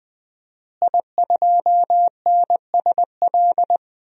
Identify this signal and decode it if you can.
Morse code, heard as a keyed tone, about 20 wpm, 700 Hz. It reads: I2NSL
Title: DXCC-100 Call Signs - 5 Characters - Repeated 2x - 20wpm